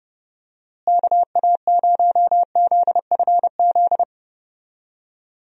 Morse code, heard as a keyed tone, about 30 wpm, 700 Hz. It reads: KA0ZFZ
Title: USA Call Signs - 6 Characters - Repeated 2x - 30wpm